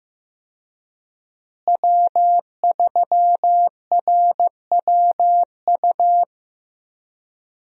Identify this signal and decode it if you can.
Morse code, heard as a keyed tone, about 15 wpm, 700 Hz. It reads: W3RWU
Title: USA Call Signs - 5 Characters - Repeated 3x - 15wpm